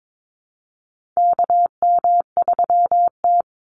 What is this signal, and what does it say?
Morse code, heard as a keyed tone, about 22 wpm, 700 Hz. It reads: KM3T
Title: Call Signs Repeated 2x - 22wpm